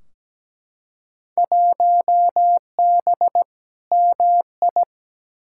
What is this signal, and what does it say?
Morse code, heard as a keyed tone, about 17 wpm, 700 Hz. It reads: 1B MI
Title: Field Day Exchanges - 17wpm